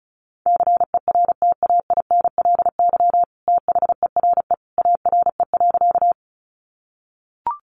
Morse code, heard as a keyed tone, about 34 wpm, 700 Hz. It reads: CERTAINLY THERE ARE.
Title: Sentences from Top 900 Words 35wpm